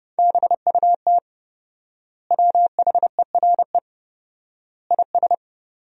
Morse code, heard as a keyed tone, about 30 wpm, 700 Hz. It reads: BUT WHERE IS
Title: 4x Word-Spacing - Sentences from Top 200 Words - 30wpm